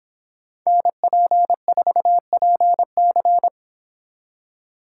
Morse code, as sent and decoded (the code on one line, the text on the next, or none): -. .--. ....- .--. -.-.
NP4PC